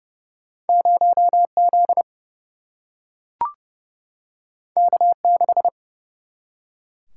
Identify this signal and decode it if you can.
Morse code, heard as a keyed tone, about 31 wpm, 700 Hz. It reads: 0 Z K 6
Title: Single-Letter Single-Number 30wpm